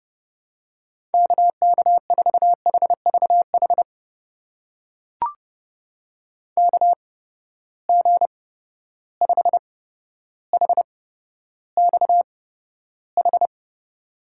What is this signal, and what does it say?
Morse code, heard as a keyed tone, about 30 wpm, spaced as for 13 wpm, 700 Hz. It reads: KK4HVH KG5HXH
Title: USA Call Signs - 6 Characters - 8x Character Spacing - 30wpm